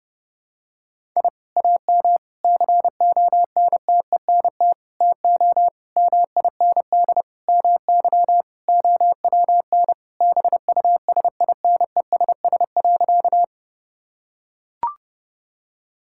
Morse code, heard as a keyed tone, about 32 wpm, 700 Hz. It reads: I AM CONTENT TO MIND MY OWN BUSINESS.
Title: Sentences from Top 1000 Words - Review Part 6 - 30wpm